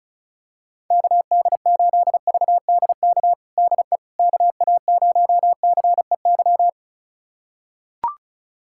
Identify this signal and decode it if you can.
Morse code, heard as a keyed tone, about 36 wpm, 700 Hz. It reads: KD8VDK DE KA0CEY
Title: USA Callsign DE Callsign 35wpm